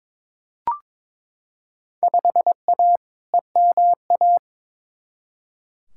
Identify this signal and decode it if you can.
Morse code, heard as a keyed tone, about 22 wpm, 700 Hz. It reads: 5A EMA
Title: Field Day Exchanges - 22wpm